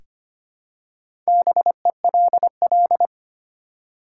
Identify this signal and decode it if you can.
Morse code, heard as a keyed tone, about 25 wpm, 690 Hz. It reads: BELL
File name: practice / 25wpm / Easy Words with Hints - 25wpm